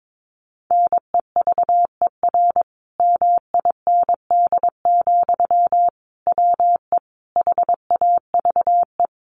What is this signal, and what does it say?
Morse code, heard as a keyed tone, about 22 wpm, 700 Hz. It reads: NEVER MIND, WE HAVE
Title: Sentences from Top 1300 Words 22wpm